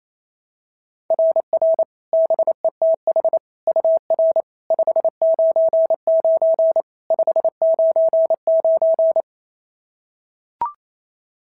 Morse code, heard as a keyed tone, about 25 wpm, 650 Hz. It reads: RR BETH UR 599 599
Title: SOTA QSO Practice 28wpm